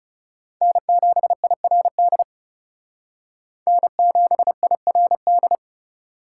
Morse code, heard as a keyed tone, about 39 wpm, 700 Hz. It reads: N7IRD N7IRD